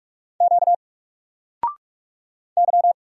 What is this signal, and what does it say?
Morse code, heard as a keyed tone, about 44 wpm, 700 Hz. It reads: Q Y
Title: Single Letter-Number Alt 45wpm